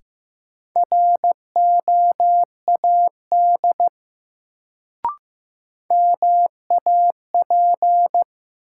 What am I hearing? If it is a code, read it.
Morse code, heard as a keyed tone, about 15 wpm, 700 Hz. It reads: ROAD MAP
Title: Sentences from Top 400 Words - New Word Drill - 15wpm